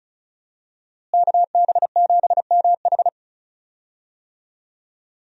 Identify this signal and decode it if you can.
Morse code, heard as a keyed tone, about 36 wpm, 700 Hz. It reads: KB7MH
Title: USA Call Signs - 5 Characters - Repeated 3x - 35wpm